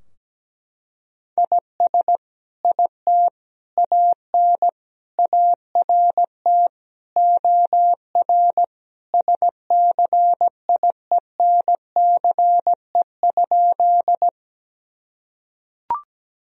Morse code, heard as a keyed tone, about 17 wpm, 700 Hz. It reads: IS IT AN ART OR SCIENCE?
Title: Sentences from Top 1000 Words - Review Part 3 - 17wpm